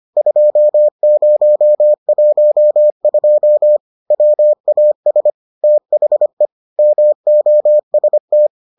Morse code, heard as a keyed tone, about 25 wpm, 600 Hz. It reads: 2012 WAS THE MOST